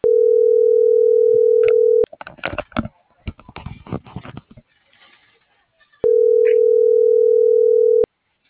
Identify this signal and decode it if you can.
Ringback tone (North America), 440 + 480 Hz.